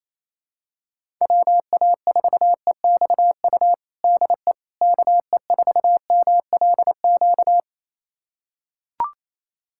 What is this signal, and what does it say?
Morse code, heard as a keyed tone, about 27 wpm, 700 Hz. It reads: WA4EXU DE KE4MLQ
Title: USA Callsign DE Callsign - 6 Characters - 28wpm